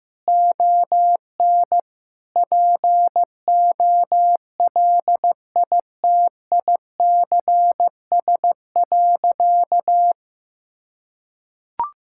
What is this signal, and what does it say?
Morse code, heard as a keyed tone, about 15 wpm, 700 Hz. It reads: ON POLITICS.
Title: Sentences from Top 1000 Words - Review Part 6 - 15wpm